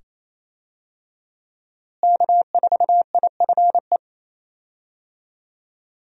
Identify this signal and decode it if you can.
Morse code, heard as a keyed tone, about 28 wpm, 700 Hz. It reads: K4IFE
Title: USA Call Signs - 5 Characters - Repeated 2x - 28wpm